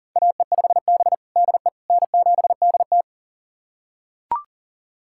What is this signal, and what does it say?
Morse code, heard as a keyed tone, about 40 wpm, 700 Hz. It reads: AE5B DE N7DT
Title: USA Callsign DE Callsign - 4 Characters - 40wpm